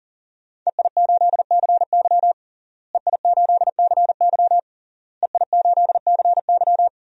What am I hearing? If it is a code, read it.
Morse code, heard as a keyed tone, about 40 wpm, 700 Hz. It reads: EI8CY EI8CY EI8CY
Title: DXCC-100 Call Signs - 5 Characters - Repeated 3x - 40wpm